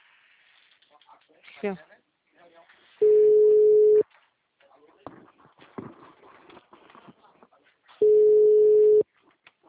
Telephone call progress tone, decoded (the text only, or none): ringback tone